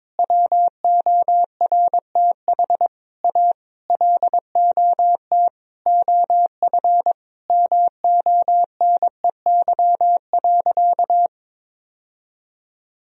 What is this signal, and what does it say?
Morse code, heard as a keyed tone, about 22 wpm, 700 Hz. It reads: WORTH A LOT OF MONEY.
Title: Sentences from Top 1000 Words - Review Part 4 - 22wpm